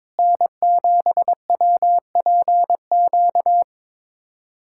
Morse code, heard as a keyed tone, about 22 wpm, 700 Hz. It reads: N7WPQ
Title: USA Callsign DE Callsign 22wpm